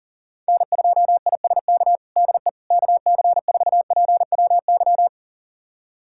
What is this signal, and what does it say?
Morse code, heard as a keyed tone, about 40 wpm, 700 Hz. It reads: N2ISK DE KC4PWY
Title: USA Callsign DE Callsign 40wpm